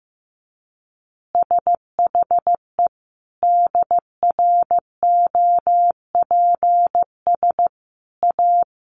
Morse code, heard as a keyed tone, about 15 wpm, 700 Hz. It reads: SHE DROPS A
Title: Easy Phrases 15wpm